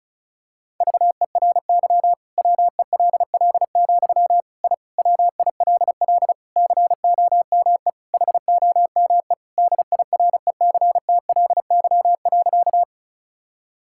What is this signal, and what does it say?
Morse code, heard as a keyed tone, about 34 wpm, 700 Hz. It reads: VERY WELL, I WILL COME HOME DIRECTLY.
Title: Sentences from Top 1000 Words - Review Part 5 - 35wpm